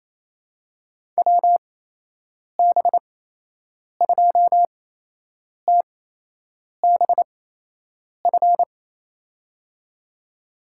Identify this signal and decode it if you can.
Morse code, heard as a keyed tone, about 28 wpm, spaced as for 8 wpm, 700 Hz. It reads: WB2TBF
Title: USA Call Signs - 8x Character Spacing - 28wpm